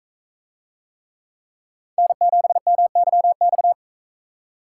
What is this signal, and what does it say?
Morse code, heard as a keyed tone, about 44 wpm, 700 Hz. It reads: N7MYX